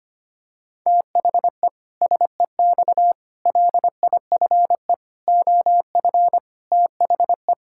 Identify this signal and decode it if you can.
Morse code, heard as a keyed tone, about 25 wpm, 700 Hz. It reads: THE SEX LIFE OF THE
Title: Pangram Challenge v2 - 25wpm